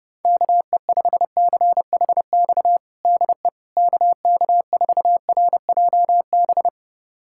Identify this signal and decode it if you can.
Morse code, heard as a keyed tone, about 30 wpm, 700 Hz. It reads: KE5CHX DE KK4RJB
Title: USA Callsign DE Callsign - 6 Characters - 30wpm